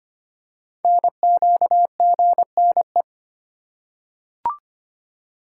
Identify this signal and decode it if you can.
Morse code, heard as a keyed tone, about 25 wpm, 700 Hz. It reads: NQGNE
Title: Code Groups - 5 Letters - 25wpm